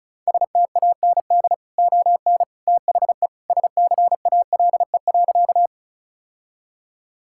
Morse code, heard as a keyed tone, about 36 wpm, 700 Hz. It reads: STAND ON THE SCALE.
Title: Sentences from Top 1000 Words - Review Part 5 - 35wpm